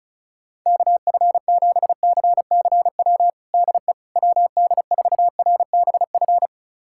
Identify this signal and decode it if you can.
Morse code, heard as a keyed tone, about 36 wpm, 700 Hz. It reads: KF7CCW DE WD4RBF
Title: USA Callsign DE Callsign - 6 Characters - 35wpm